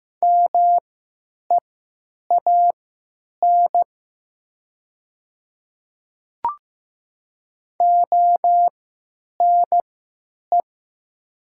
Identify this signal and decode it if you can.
Morse code, heard as a keyed tone, about 15 wpm, spaced as for 9 wpm, 700 Hz. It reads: MEAN ONE
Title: 3x Character Spacing - Top 500 Words - 15wpm